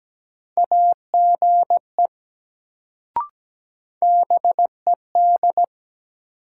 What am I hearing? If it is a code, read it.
Morse code, heard as a keyed tone, about 17 wpm, 700 Hz. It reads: AGE BED
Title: Sentences from Top 500 Words - New Word Drill - 17wpm